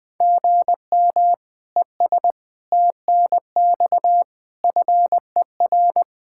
Morse code, heard as a keyed tone, about 20 wpm, 700 Hz. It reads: GM ES TNX FER
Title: Practice QSO - Exchange 1 - Beginning - 20wpm